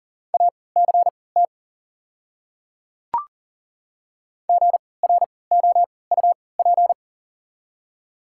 Morse code, heard as a keyed tone, about 39 wpm, spaced as for 22 wpm, 700 Hz. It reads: ACT GROUP
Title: 3x Character Spacing - Top 300 Words - 40wpm